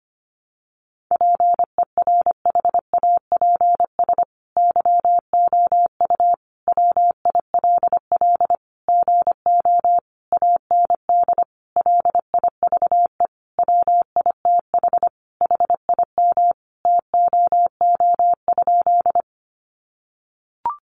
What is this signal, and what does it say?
Morse code, heard as a keyed tone, about 25 wpm, 700 Hz. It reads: PERHAPS YOU WILL GO AND LIVE WITH HIM TOO?
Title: Sentences from Top 500 Words 25wpm